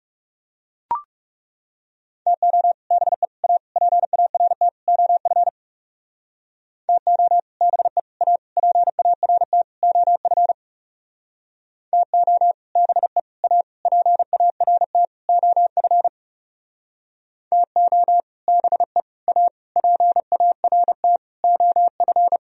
Morse code, heard as a keyed tone, about 38 wpm, 700 Hz. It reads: TO BE A PART OF TO BE A PART OF TO BE A PART OF TO BE A PART OF